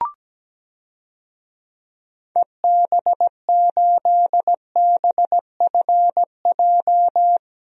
Morse code, heard as a keyed tone, about 17 wpm, 700 Hz. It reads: EB8BFJ
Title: DXCC-100 Call Signs - 6 Characters - 17wpm